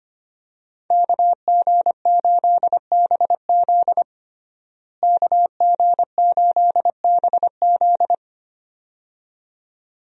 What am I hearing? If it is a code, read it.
Morse code, heard as a keyed tone, about 25 wpm, 700 Hz. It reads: KG8BZ KG8BZ
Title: USA Call Signs - 5 Characters - Repeated 2x - 25wpm